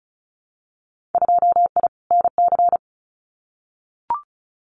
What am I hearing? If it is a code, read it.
Morse code, heard as a keyed tone, about 34 wpm, 700 Hz. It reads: 2I NC
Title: Winter Field Day Exchanges 35wpm